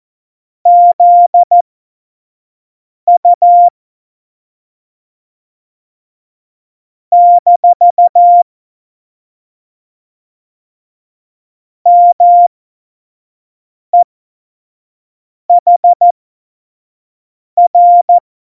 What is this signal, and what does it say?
Morse code, heard as a keyed tone, about 14 wpm, spaced as for 5 wpm, 700 Hz. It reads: ZU - MEHR
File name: Morsekurs / Witze / Witz 06. Oktober 2020, 25 F70